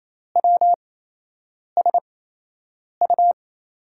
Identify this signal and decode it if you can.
Morse code, heard as a keyed tone, about 28 wpm, spaced as for 8 wpm, 700 Hz. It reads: WSU
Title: USA Call Signs - 5 Characters - 8x Character Spacing - 28wpm